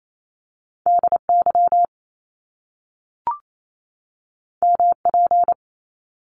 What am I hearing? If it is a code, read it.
Morse code, heard as a keyed tone, about 28 wpm, 700 Hz. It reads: DY MP
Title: Top Two Letter Combinations in Words 28wpm